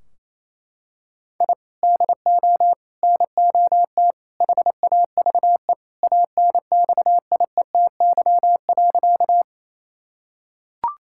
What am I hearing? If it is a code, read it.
Morse code, heard as a keyed tone, about 28 wpm, 700 Hz. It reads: I DO NOT HAVE ANXIETY.
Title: Sentences from Top 1600 Words 28wpm